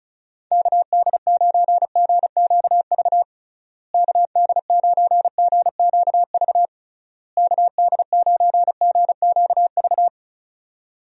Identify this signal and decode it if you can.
Morse code, heard as a keyed tone, about 34 wpm, 700 Hz. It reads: KD9GQV KD9GQV KD9GQV
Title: USA Call Signs - 6 Characters - Repeated 3x - 35wpm